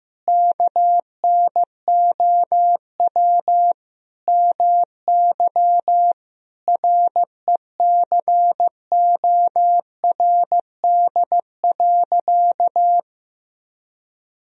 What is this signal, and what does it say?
Morse code, heard as a keyed tone, about 15 wpm, 700 Hz. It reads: KNOW MY RECORD.
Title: Sentences from Top 1000 Words - Review Part 3 - 15wpm